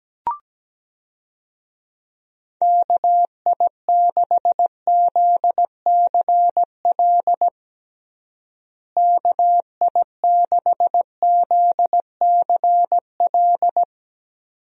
Morse code, heard as a keyed tone, about 17 wpm, 700 Hz. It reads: KI6ZCL KI6ZCL
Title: USA Call Signs - 6 Characters - Repeated 2x - 17wpm